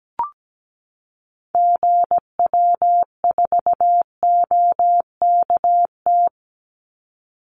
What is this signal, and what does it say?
Morse code, heard as a keyed tone, about 17 wpm, 700 Hz. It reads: GW4OKT
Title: Call Signs - 17wpm